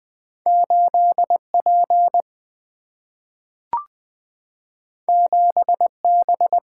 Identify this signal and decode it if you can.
Morse code, heard as a keyed tone, about 20 wpm, 700 Hz. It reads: 8 P 7 B
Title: Single-Letter Single-Number 20wpm